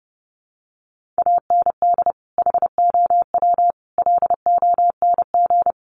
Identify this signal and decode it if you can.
Morse code, heard as a keyed tone, about 28 wpm, 700 Hz. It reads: AND HOW LONG
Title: Sentences from Top 1500 Words 30wpm